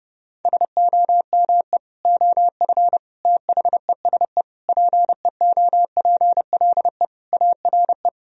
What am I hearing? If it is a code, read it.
Morse code, heard as a keyed tone, about 30 wpm, 700 Hz. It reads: SOME OF THESE PEOPLE ARE
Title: Sentences from Top 1000 Words - Review Part 3 - 30wpm